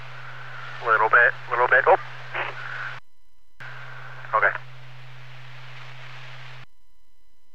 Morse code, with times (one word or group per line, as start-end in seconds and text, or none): none